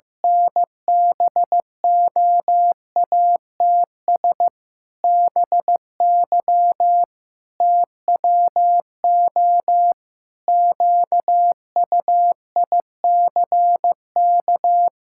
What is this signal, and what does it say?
Morse code, heard as a keyed tone, about 16 wpm, 700 Hz. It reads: N BOATS BY TWO QUICK
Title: Pangram Challenge v2 - 15wpm